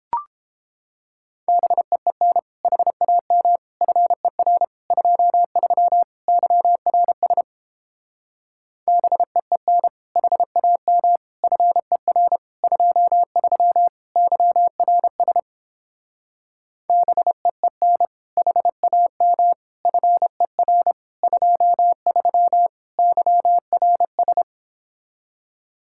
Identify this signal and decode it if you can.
Morse code, heard as a keyed tone, about 31 wpm, 700 Hz. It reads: BEEN HAM FER 23 YRS BEEN HAM FER 23 YRS BEEN HAM FER 23 YRS